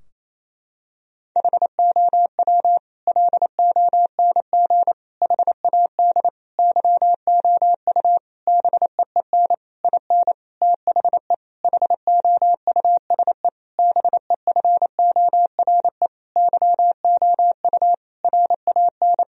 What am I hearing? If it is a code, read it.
Morse code, heard as a keyed tone, about 28 wpm, 700 Hz. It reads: HOW LONG HAD YOU BEEN IN THE HOUSE BEFORE YOU RAN
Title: Sentences from Top 1000 Words - Review Part 3 - 28wpm